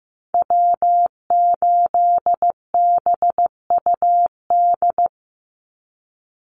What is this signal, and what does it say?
Morse code, heard as a keyed tone, about 15 wpm, 700 Hz. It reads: W8BUD
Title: Call Signs Repeated 2x - 15wpm